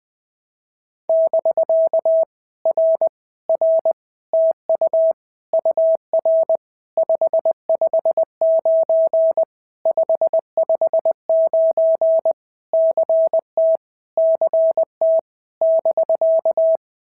Morse code, heard as a keyed tone, about 21 wpm, 650 Hz. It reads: <BK> R R TU UR 559 559 CT CT <BK>
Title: POTA QSO Practice - 20wpm